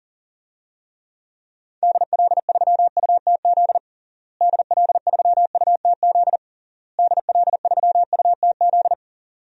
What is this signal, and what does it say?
Morse code, heard as a keyed tone, about 40 wpm, 700 Hz. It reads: DL3UTZ DL3UTZ DL3UTZ
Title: DXCC-100 Call Signs - 6 Characters - Repeated 3x - 40wpm